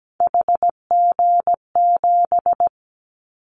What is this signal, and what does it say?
Morse code, heard as a keyed tone, about 17 wpm, 700 Hz. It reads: HG7
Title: Callsign Prefixes for Top 25 DXCC - 17wpm